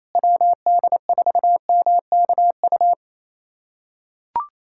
Morse code, heard as a keyed tone, about 28 wpm, 700 Hz. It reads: WD4MKU
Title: USA Callsign DE Callsign - 6 Characters - 28wpm